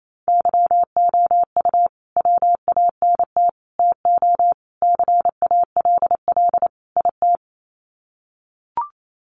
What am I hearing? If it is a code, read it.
Morse code, heard as a keyed tone, about 28 wpm, 700 Hz. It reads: YOU WANT TO CALL IT
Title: Sets of 5 Words Encore2 28wpm